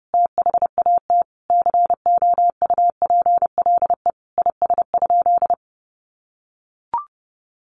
Morse code, heard as a keyed tone, about 30 wpm, 700 Hz. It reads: THAT COUPLE IS?
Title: Sentences from Top 1100 Words 30wpm